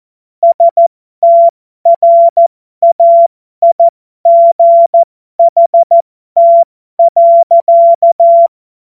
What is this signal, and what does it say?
Morse code, heard as a keyed tone, about 14 wpm, spaced as for 12 wpm, 680 Hz. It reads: STRAIGHT.
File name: Morsekurs / Witze